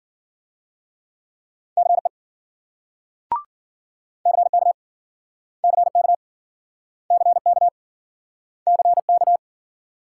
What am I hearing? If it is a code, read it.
Morse code, heard as a keyed tone, about 52 wpm, 700 Hz. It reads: CE CK CK CK CK